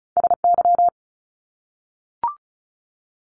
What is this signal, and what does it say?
Morse code, heard as a keyed tone, about 36 wpm, 700 Hz. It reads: SY
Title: Top Two Letter Combinations in Words 35wpm